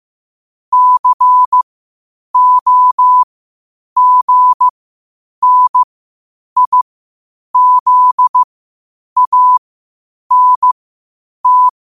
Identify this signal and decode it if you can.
Morse code, heard as a keyed tone, about 15 wpm, spaced as for 8 wpm, 1000 Hz. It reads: COGNIZANT
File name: qotdfiles / 8wpm